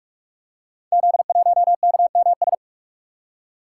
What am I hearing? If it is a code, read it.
Morse code, heard as a keyed tone, about 44 wpm, 700 Hz. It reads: Z1KMS
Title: Code Groups - 5 Letter:Numbers - 45wpm